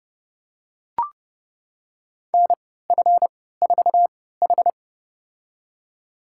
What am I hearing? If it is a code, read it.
Morse code, heard as a keyed tone, about 29 wpm, spaced as for 17 wpm, 700 Hz. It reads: NF4H
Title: USA Call Signs - 3x Character Spacing - 30wpm